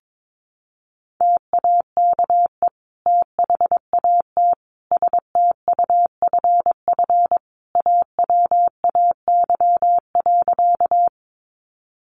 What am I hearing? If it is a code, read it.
Morse code, heard as a keyed tone, about 22 wpm, 700 Hz. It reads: TAKE THAT STUFF AWAY.
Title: Sentences from Top 1400 Words 22wpm